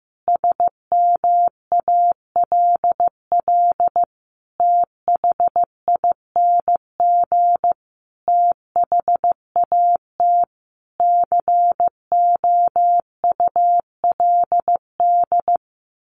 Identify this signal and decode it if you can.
Morse code, heard as a keyed tone, about 15 wpm, 700 Hz. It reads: SMALL THING THAT COULD
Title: A Sound Of Thunder 15wpm